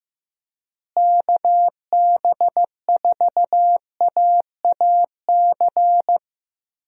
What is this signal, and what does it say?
Morse code, heard as a keyed tone, about 15 wpm, 700 Hz. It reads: KB4AAC
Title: USA Call Signs - 6 Characters - Repeated 2x - 15wpm